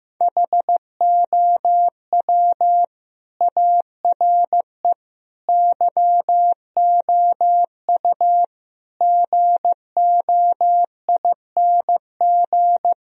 Morse code, heard as a keyed tone, about 15 wpm, 700 Hz. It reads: HOW ARE YOU GOING
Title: Speed-Racing - Sets of 5 Words 15wpm